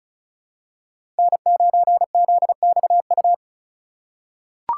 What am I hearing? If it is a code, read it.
Morse code, heard as a keyed tone, about 34 wpm, 700 Hz. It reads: N9ZXU
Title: USA Call Signs - 2x Character Spacing - 35wpm